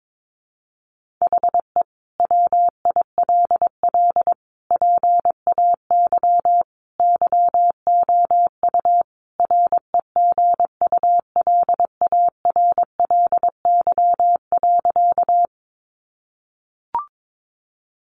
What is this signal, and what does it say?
Morse code, heard as a keyed tone, about 22 wpm, 700 Hz. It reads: HE WILL PAY YOU REGULARLY.
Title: Sentences from Top 1400 Words 22wpm